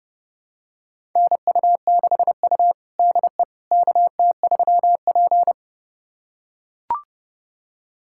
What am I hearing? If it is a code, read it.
Morse code, heard as a keyed tone, about 30 wpm, 700 Hz. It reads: NU6U DE KT3P
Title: USA Callsign DE Callsign - 4 Characters - 30wpm